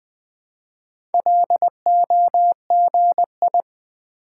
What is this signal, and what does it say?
Morse code, heard as a keyed tone, about 20 wpm, 700 Hz. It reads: LOGI
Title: Build A Word - 1 Letter At A Time - 3 to 5-Letter Words - 20wpm